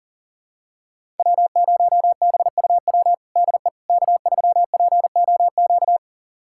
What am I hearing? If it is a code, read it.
Morse code, heard as a keyed tone, about 40 wpm, 700 Hz. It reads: W0BUW DE K3POQ
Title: USA Callsign DE Callsign - 5 Characters - 40wpm